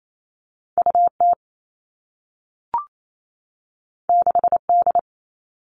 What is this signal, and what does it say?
Morse code, heard as a keyed tone, about 28 wpm, 700 Hz. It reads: U T 6 D
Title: Two Challenging Letter-Number Combinations 28wpm